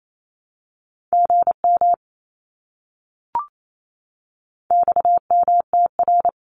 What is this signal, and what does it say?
Morse code, heard as a keyed tone, about 28 wpm, 700 Hz. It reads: GM XMTR
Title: Abbreviations Repeated 3x - 28wpm